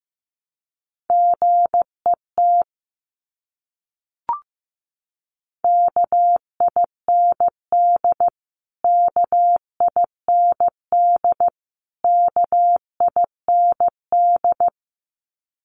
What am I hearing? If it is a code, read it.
Morse code, heard as a keyed tone, about 15 wpm, 700 Hz. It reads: GET KIND KIND KIND
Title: Top 500 Words Repeated 15wpm